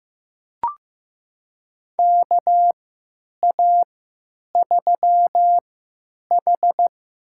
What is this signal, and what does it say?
Morse code, heard as a keyed tone, about 15 wpm, spaced as for 9 wpm, 700 Hz. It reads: KA3H
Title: USA Call Signs - 6 Characters - 3x Character Spacing - 15wpm